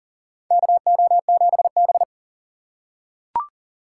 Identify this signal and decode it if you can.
Morse code, heard as a keyed tone, about 40 wpm, 700 Hz. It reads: KO7B
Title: USA Call Signs - 6x Character Spacing - 40wpm